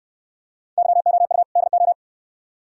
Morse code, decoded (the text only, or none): YOU DO